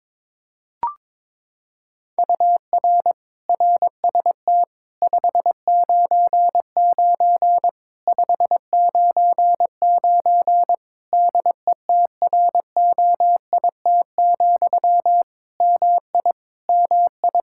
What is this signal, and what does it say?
Morse code, heard as a keyed tone, about 22 wpm, 700 Hz. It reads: UR RST 599 599 DETROIT, MI MI
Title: Practice QSO - Exchange 1 - Beginning - 22wpm